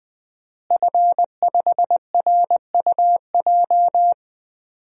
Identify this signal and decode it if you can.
Morse code, heard as a keyed tone, about 20 wpm, 700 Hz. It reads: F5RUJ
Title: DXCC-100 Call Signs - 5 Characters - Repeated 3x - 20wpm